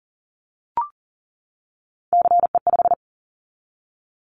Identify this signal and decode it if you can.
Morse code, heard as a keyed tone, about 41 wpm, 700 Hz. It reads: CE5
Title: Callsign Prefixes for Top 100 DXCC - 40wpm